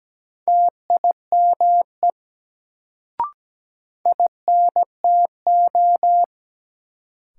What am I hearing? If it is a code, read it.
Morse code, heard as a keyed tone, about 17 wpm, 700 Hz. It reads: TIME INTO
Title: Sets of 2 Words - New Word Drill - 17wpm